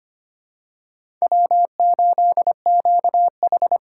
Morse code, heard as a keyed tone, about 25 wpm, 700 Hz. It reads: W8QH
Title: USA Call Signs - 4 Characters - 2x Character Spacing - 25wpm